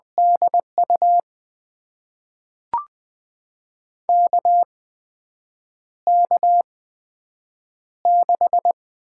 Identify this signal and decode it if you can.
Morse code, heard as a keyed tone, about 20 wpm, spaced as for 7 wpm, 700 Hz. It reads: DU KK6
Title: USA Call Signs - 6 Characters - 8x Character Spacing - 20wpm